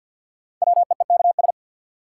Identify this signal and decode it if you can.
Morse code, heard as a keyed tone, about 47 wpm, 700 Hz. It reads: WEEKS
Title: Sets of 5 Words - New Word Drill - 50wpm